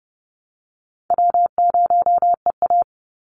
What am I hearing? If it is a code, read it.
Morse code, heard as a keyed tone, about 29 wpm, 700 Hz. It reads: W0EA
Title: Easy Call Signs Repeated 3x - 30wpm